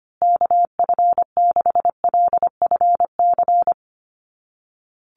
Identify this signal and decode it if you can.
Morse code, heard as a keyed tone, about 25 wpm, 700 Hz. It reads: KF6LFC
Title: USA Call Signs Repeated 3x 25wpm